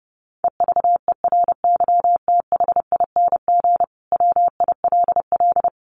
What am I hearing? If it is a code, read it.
Morse code, heard as a keyed tone, about 30 wpm, 700 Hz. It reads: EVERYTHING WILL
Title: Sentences from Top 1300 Words 30wpm